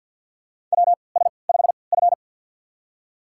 Morse code, heard as a keyed tone, about 52 wpm, spaced as for 28 wpm, 700 Hz. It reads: WS5F
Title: USA Call Signs - 3x Character Spacing - 50wpm